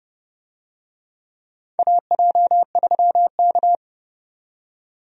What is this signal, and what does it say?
Morse code, heard as a keyed tone, about 30 wpm, 700 Hz. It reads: AJ3K
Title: USA Call Signs - 4 Characters - 30wpm